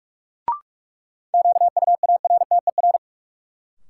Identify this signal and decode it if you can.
Morse code, heard as a keyed tone, about 41 wpm, 700 Hz. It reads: QUARTER
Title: Sentences from Top 1300 Words - New Word Drill - 45wpm